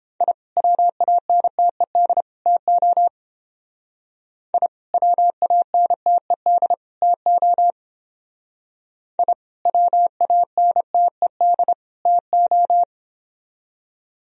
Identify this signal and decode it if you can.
Morse code, heard as a keyed tone, about 29 wpm, 700 Hz. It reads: I WANTED TO I WANTED TO I WANTED TO